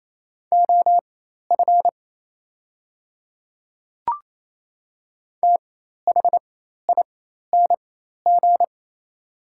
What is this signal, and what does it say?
Morse code, heard as a keyed tone, about 28 wpm, spaced as for 13 wpm, 700 Hz. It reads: OF THING
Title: 4x Character Spacing - Top 200 Words - 28wpm